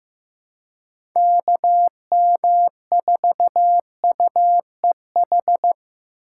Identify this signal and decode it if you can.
Morse code, heard as a keyed tone, about 15 wpm, 700 Hz. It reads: KM4UEH
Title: USA Call Signs - 6 Characters - 3x Character Spacing - 15wpm